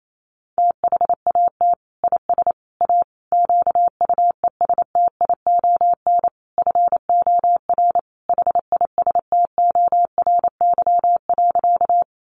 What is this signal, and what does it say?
Morse code, heard as a keyed tone, about 28 wpm, 700 Hz. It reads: THAT IS A QUESTION FOR HISTORY.
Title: Sentences from Top 600 Words 28wpm